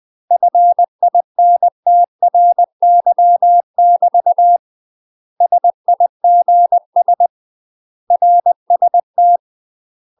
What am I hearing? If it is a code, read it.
Morse code, heard as a keyed tone, about 20 wpm, 700 Hz. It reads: FINTRY <BT> SIGS RST
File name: UK MASTER QSOs TRAINING / 20WPM / UK MASTER QSOs 0025